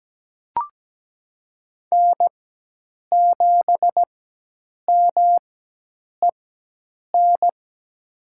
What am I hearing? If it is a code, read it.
Morse code, heard as a keyed tone, about 17 wpm, spaced as for 8 wpm, 700 Hz. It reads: N7MEN
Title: USA Call Signs - 4x Character Spacing - 17wpm